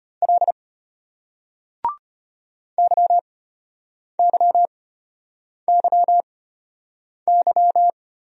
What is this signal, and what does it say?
Morse code, heard as a keyed tone, about 34 wpm, spaced as for 8 wpm, 700 Hz. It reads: L YYYY